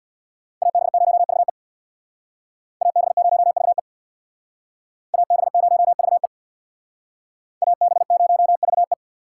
Morse code, heard as a keyed tone, about 62 wpm, 700 Hz. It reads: AB0VE AB0VE AB0VE AB0VE
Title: Speed Racing - USA Call Signs 50wpm